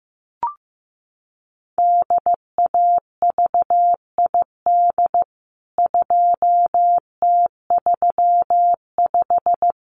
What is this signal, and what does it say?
Morse code, heard as a keyed tone, about 15 wpm, 700 Hz. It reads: DAVID 2T35
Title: CWT Exchange 15wpm